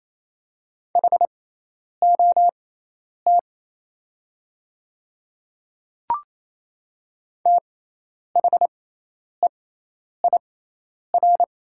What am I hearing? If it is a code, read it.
Morse code, heard as a keyed tone, about 27 wpm, spaced as for 10 wpm, 700 Hz. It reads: HOT THEIR
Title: 6x Character Spacing - Top 200 Words - 28wpm